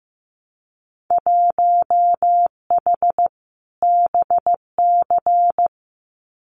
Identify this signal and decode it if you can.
Morse code, heard as a keyed tone, about 15 wpm, 700 Hz. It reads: 1H BC
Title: Winter Field Day Exchanges Encore 15wpm